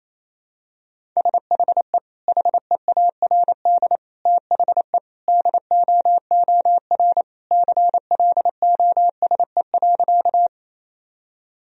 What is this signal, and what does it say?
Morse code, heard as a keyed tone, about 28 wpm, 700 Hz. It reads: SHE HEARD THE DOOR CLOSE.
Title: Sentences from Top 1000 Words - Review Part 1 - 28wpm